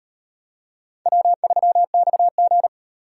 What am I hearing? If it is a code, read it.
Morse code, heard as a keyed tone, about 37 wpm, 700 Hz. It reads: W3XG